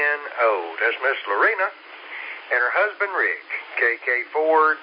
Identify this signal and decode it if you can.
Emergency Alert System (SAME) tones, heard as harsh buzzing bursts, with speech. It is heard from 0.7 to 3.7 s.